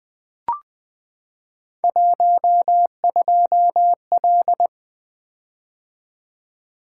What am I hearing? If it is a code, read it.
Morse code, heard as a keyed tone, about 20 wpm, 700 Hz. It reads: 12L
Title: Three Letter-Number Combinations - 20wpm